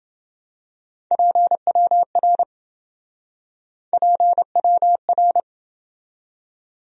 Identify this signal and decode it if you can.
Morse code, heard as a keyed tone, about 27 wpm, 700 Hz. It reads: PWR PWR